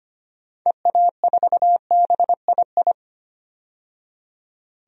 Morse code, heard as a keyed tone, about 25 wpm, 700 Hz. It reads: EA4BII
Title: DXCC-100 Call Signs - 6 Characters - 25wpm